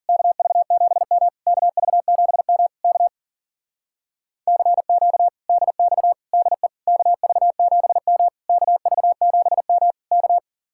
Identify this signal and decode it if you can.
Morse code, heard as a keyed tone, about 47 wpm, 700 Hz. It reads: KV7M KV7M K CQ DX DE KV7M KV7M K
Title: Speed-Racing Common QSO Phrases - 40wpm